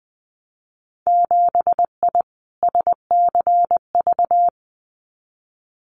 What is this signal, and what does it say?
Morse code, heard as a keyed tone, about 20 wpm, 700 Hz. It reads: 7I SCV
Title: Winter Field Day Exchanges Encore 20wpm